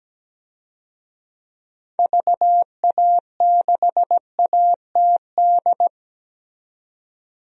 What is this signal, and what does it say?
Morse code, heard as a keyed tone, about 17 wpm, 700 Hz. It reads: VA6ATD
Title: DXCC-100 Call Signs - 6 Characters - 17wpm